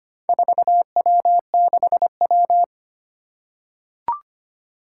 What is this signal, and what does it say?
Morse code, heard as a keyed tone, about 25 wpm, 700 Hz. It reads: 4W6W
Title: Four Challenging Letter-Number Combinations 25wpm